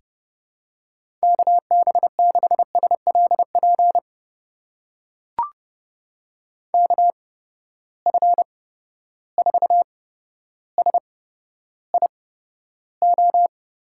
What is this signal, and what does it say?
Morse code, heard as a keyed tone, about 30 wpm, spaced as for 13 wpm, 700 Hz. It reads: KB6SLP KF4SIO
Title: USA Call Signs - 6 Characters - 8x Character Spacing - 30wpm